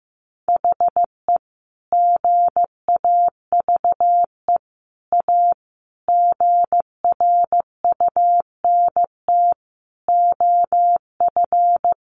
Morse code, heard as a keyed tone, about 15 wpm, 700 Hz. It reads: HE GAVE A GRUNT OF
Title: A Sound Of Thunder 15wpm